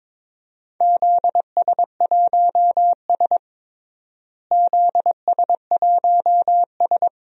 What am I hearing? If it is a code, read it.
Morse code, heard as a keyed tone, about 22 wpm, 700 Hz. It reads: ZS1S ZS1S
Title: DXCC-100 Call Signs - 4 Characters - Repeated 2x - 22wpm